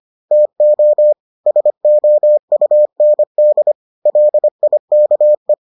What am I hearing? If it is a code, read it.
Morse code, heard as a keyed tone, about 25 wpm, 600 Hz. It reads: TO SOUND LIKE